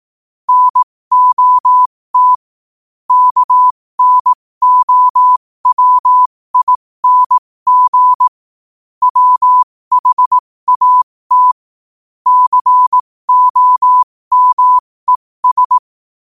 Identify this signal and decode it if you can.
Morse code, heard as a keyed tone, about 18 wpm, spaced as for 15 wpm, 1000 Hz. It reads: NOT KNOWING WHAT COMES
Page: qotd-at-15-wpm